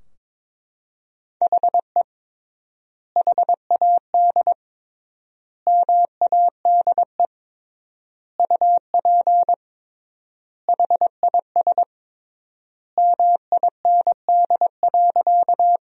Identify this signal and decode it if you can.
Morse code, heard as a keyed tone, about 22 wpm, 700 Hz. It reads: HE HAD MADE UP HIS MIND.
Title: 3x Word-Spacing - Sentences from Top 500 Words - 22wpm